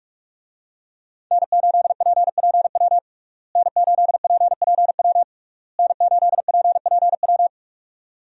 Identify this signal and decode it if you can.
Morse code, heard as a keyed tone, about 43 wpm, 700 Hz. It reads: N8PPW N8PPW N8PPW
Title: USA Call Signs - 5 Characters - Repeated 3x - 45wpm